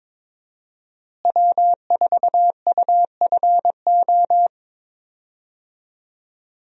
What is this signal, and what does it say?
Morse code, heard as a keyed tone, about 23 wpm, 700 Hz. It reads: W4UFO
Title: USA Call Signs - 5 Characters - Repeated 2x - 22wpm